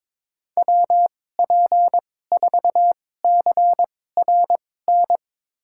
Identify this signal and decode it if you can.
Morse code, heard as a keyed tone, about 22 wpm, spaced as for 16 wpm, 700 Hz. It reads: WP4CRN
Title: USA Call Signs - 6 Characters - 2x Character Spacing - 22wpm